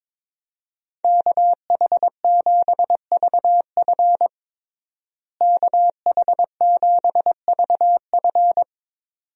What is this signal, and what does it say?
Morse code, heard as a keyed tone, about 22 wpm, 700 Hz. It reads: KH7VF KH7VF
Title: USA Call Signs - 5 Characters - Repeated 2x - 22wpm